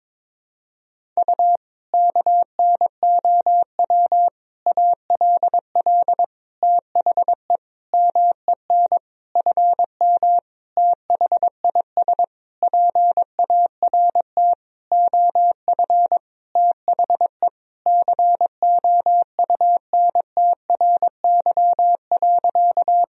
Morse code, heard as a keyed tone, about 22 wpm, 700 Hz. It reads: U KNOW ALL THE MEN FM THIS PART OF THE COUNTRY.
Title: Sentences from Top 300 Words and Abbreviations 22wpm